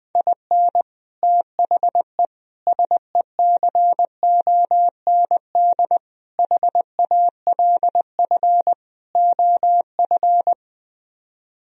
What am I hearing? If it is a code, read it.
Morse code, heard as a keyed tone, about 21 wpm, 700 Hz. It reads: IN THE SECOND HALF OF
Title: Speed-Racing - Sets of 5 Words 20wpm